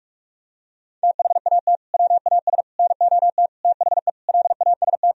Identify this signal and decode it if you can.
Morse code, heard as a keyed tone, about 41 wpm, 700 Hz. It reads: THAT WAS NOT THE LAST
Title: Sentences from Top 1000 Words - Review Part 4 - 45wpm